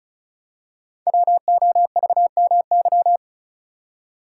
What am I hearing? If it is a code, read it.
Morse code, heard as a keyed tone, about 36 wpm, 690 Hz. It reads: WOVMY
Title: Code Groups - 5 Letters - 35wpm